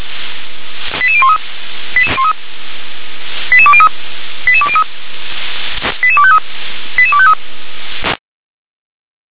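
Five-tone selective call (ZVEI), heard as quick runs of five tones, tones of about 70 ms. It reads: 88013 88203 88203 88253 88253
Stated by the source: ZVEI five-tone call received by SDR.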